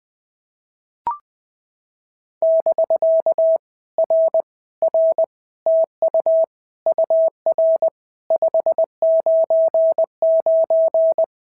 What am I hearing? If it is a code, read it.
Morse code, heard as a keyed tone, about 20 wpm, 650 Hz. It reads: <BK> R R TU UR 599
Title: POTA QSO Practice - 20wpm